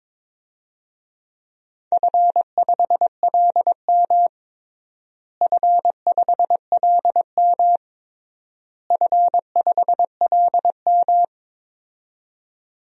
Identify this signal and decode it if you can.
Morse code, heard as a keyed tone, about 22 wpm, 700 Hz. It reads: F5LM F5LM F5LM
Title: DXCC-100 Call Signs - 4 Characters - Repeated 3x - 22wpm